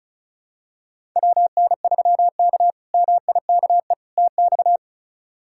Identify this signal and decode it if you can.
Morse code, heard as a keyed tone, about 34 wpm, 700 Hz. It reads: WN3K MIKE TX
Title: K1USN SST QSO Practice - 35wpm